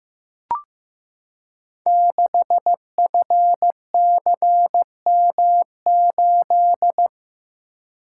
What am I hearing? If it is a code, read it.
Morse code, heard as a keyed tone, about 15 wpm, 700 Hz. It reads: 6FCM8
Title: Code Groups - 5 Letter:Numbers - 15wpm